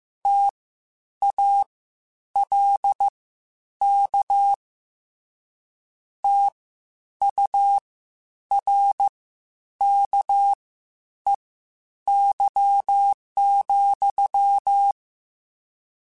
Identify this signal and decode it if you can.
Morse code, heard as a keyed tone, about 15 wpm, spaced as for 9 wpm, 800 Hz. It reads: TALK TURKEY,
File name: Misc / Misc-mp3s / WordPairs sp @13